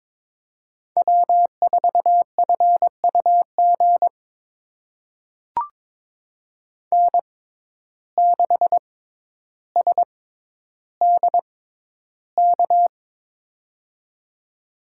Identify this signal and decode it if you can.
Morse code, heard as a keyed tone, about 22 wpm, spaced as for 11 wpm, 700 Hz. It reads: W4FUG N6SDK
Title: USA Call Signs - 5 Characters - 6x Character Spacing - 22wpm